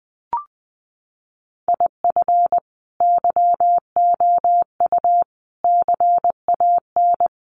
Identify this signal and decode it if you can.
Morse code, heard as a keyed tone, about 20 wpm, 700 Hz. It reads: IF YOU CAN
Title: Sets of 4 Words 20wpm